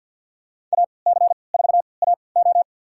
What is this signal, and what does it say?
Morse code, heard as a keyed tone, about 49 wpm, spaced as for 28 wpm, 700 Hz. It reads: AC4AO
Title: USA Call Signs - 5 Characters - 3x Character Spacing - 50wpm